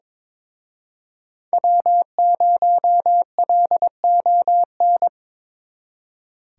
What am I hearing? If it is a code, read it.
Morse code, heard as a keyed tone, about 26 wpm, 700 Hz. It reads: W0LON
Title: Speed Racing - USA 1x3 Call Signs 22wpm